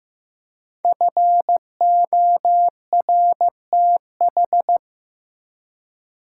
Morse code, heard as a keyed tone, about 15 wpm, 700 Hz. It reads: FORTH
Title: Five Letter Words - 15wpm